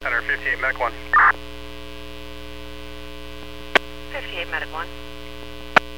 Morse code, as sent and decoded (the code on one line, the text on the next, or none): none